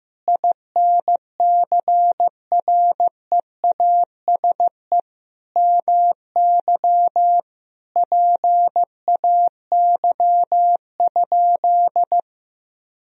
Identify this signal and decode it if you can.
Morse code, heard as a keyed tone, about 15 wpm, 700 Hz. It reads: INCREASE MY PAY?
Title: Sentences from Top 1000 Words - Review Part 4 - 15wpm